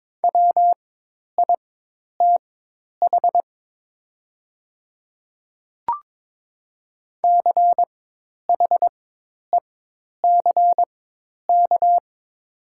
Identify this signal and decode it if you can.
Morse code, heard as a keyed tone, about 22 wpm, spaced as for 10 wpm, 700 Hz. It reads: WITH CHECK
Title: 4x Character Spacing - Top 500 Words - 22wpm